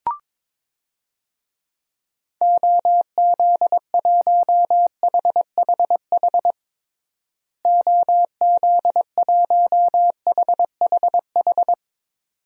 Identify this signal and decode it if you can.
Morse code, heard as a keyed tone, about 22 wpm, 700 Hz. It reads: OZ1HHH OZ1HHH
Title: DXCC-100 Call Signs - 6 Characters - Repeated 2x - 22wpm